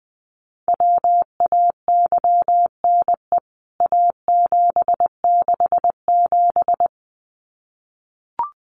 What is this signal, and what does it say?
Morse code, heard as a keyed tone, about 20 wpm, 700 Hz. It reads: WAYNE A767
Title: CWT Exchange 20wpm